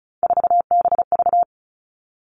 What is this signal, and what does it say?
Morse code, heard as a keyed tone, about 36 wpm, 700 Hz. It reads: 4BV
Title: Three Challenging Letter-Number Combinations 35wpm